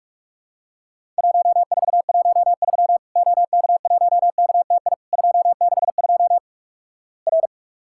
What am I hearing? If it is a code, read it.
Morse code, heard as a keyed tone, about 48 wpm, 700 Hz. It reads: 1413 OK1KTI 262 R
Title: Worked All Europe DX Contest - QTC Exchange - 45wpm